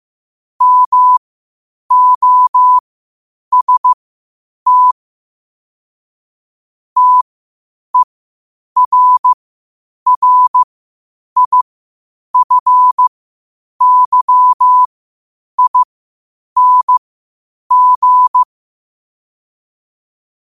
Morse code, heard as a keyed tone, about 15 wpm, spaced as for 8 wpm, 1000 Hz. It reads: MOST TERRIFYING
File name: qotdfiles / 8wpm